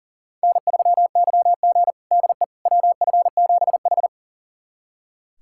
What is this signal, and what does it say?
Morse code, heard as a keyed tone, about 40 wpm, 700 Hz. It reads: N3YG DE WF7H
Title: USA Callsign DE Callsign - 4 Characters - 40wpm